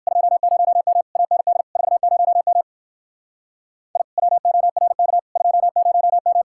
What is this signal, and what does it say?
Morse code, heard as a keyed tone, about 58 wpm, 700 Hz. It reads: 20M AND 40M I WORK 20M